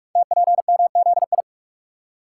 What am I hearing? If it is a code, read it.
Morse code, heard as a keyed tone, about 44 wpm, 700 Hz. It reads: TPMZI
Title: Code Groups - 5 Letters - 45wpm